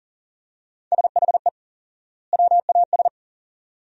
Morse code, heard as a keyed tone, about 40 wpm, 700 Hz. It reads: SHE WAS
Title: 4x Word-Spacing - Sentences from Top 400 Words - 40wpm